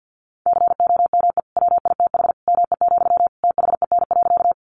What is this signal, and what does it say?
Morse code, heard as a keyed tone, about 50 wpm, 700 Hz. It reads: COME WITH ME, THEN.
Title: Sentences from Top 200 Words 50wpm